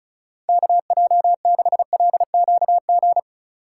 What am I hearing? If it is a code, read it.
Morse code, heard as a keyed tone, about 36 wpm, 700 Hz. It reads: KJ6LQG
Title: USA Call Signs - 6 Characters - 4x Character Spacing - 35wpm